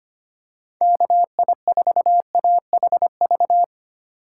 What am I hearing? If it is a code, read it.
Morse code, heard as a keyed tone, about 25 wpm, 700 Hz. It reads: KI4AHV
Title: USA Call Signs - 6 Characters - 3x Character Spacing - 25wpm